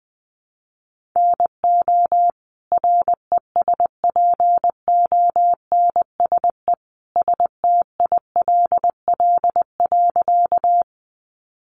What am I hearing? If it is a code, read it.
Morse code, heard as a keyed tone, about 20 wpm, 700 Hz. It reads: NO RESPONSE STILL.
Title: Sentences from Top 1400 Words 20wpm